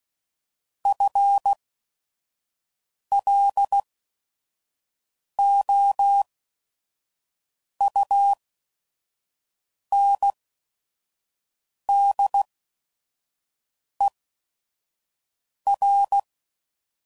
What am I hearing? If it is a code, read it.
Morse code, heard as a keyed tone, about 16 wpm, spaced as for 5 wpm, 800 Hz. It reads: FLOUNDER